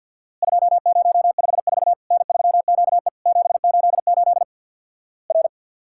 Morse code, heard as a keyed tone, about 50 wpm, 700 Hz. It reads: 2054 N3QE 788 R
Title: Worked All Europe DX Contest - QTC Exchange with Repeats - 50wpm